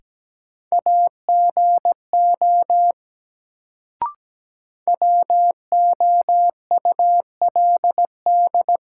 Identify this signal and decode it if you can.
Morse code, heard as a keyed tone, about 17 wpm, 700 Hz. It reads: AGO WOULD
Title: Sets of 2 Words - New Word Drill - 17wpm